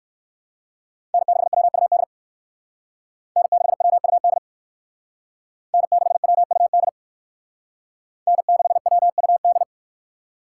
Morse code, heard as a keyed tone, about 58 wpm, 700 Hz. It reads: N6WUD N6WUD N6WUD N6WUD